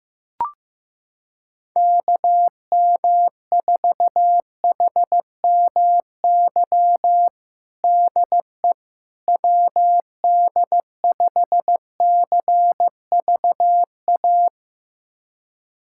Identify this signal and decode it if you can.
Morse code, heard as a keyed tone, about 15 wpm, 700 Hz. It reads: KM4HMY DE WD5CVA
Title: USA Callsign DE Callsign - 6 Characters - 15wpm